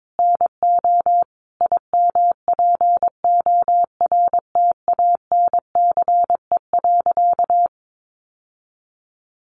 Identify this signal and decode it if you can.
Morse code, heard as a keyed tone, about 22 wpm, 700 Hz. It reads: NO IMPORTANCE.
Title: Sentences from Top 1200 Words 22wpm